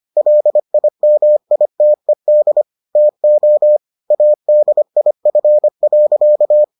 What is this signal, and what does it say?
Morse code, heard as a keyed tone, about 25 wpm, 600 Hz. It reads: LIMITED TO ADIF.